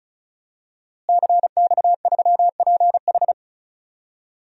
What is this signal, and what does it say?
Morse code, heard as a keyed tone, about 34 wpm, 700 Hz. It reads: CX3PH
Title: DXCC-100 Call Signs - 5 Characters - Repeated 3x - 35wpm